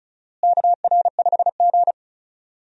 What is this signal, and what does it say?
Morse code, heard as a keyed tone, about 34 wpm, 700 Hz. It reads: KR5G
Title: USA Call Signs - 4 Characters - Repeated 2x - 35wpm